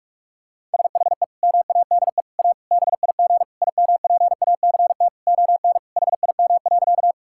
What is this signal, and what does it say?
Morse code, heard as a keyed tone, about 43 wpm, 700 Hz. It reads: SHE MADE A BIG IMPACT ON HIM.
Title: Sentences from Top 1000 Words - Review Part 6 - 45wpm